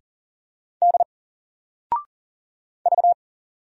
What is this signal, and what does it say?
Morse code, heard as a keyed tone, about 39 wpm, 700 Hz. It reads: D V
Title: Single Character (Letter:Number:Punctuation:Prosign) 40wpm